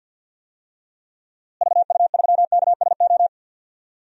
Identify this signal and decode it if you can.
Morse code, heard as a keyed tone, about 50 wpm, 700 Hz. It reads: VU3KSO
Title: DXCC-100 Call Signs - 6 Characters - Repeated 2x - 50wpm